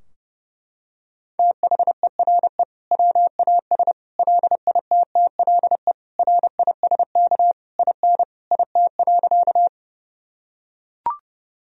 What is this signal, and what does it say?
Morse code, heard as a keyed tone, about 32 wpm, 700 Hz. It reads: THERE WAS LITTLE RISK IN IT.
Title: Sentences from Top 1000 Words - Review Part 4 - 30wpm